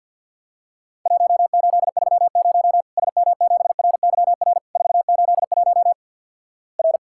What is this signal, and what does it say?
Morse code, heard as a keyed tone, about 50 wpm, 700 Hz. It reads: 1830 SM7RYR 481 R
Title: Worked All Europe DX Contest - QTC Exchange - 50wpm